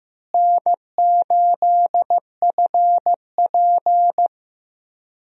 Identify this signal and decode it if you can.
Morse code, heard as a keyed tone, about 15 wpm, 700 Hz. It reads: N8FP
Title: USA Call Signs - 4 Characters - Repeated 3x - 15wpm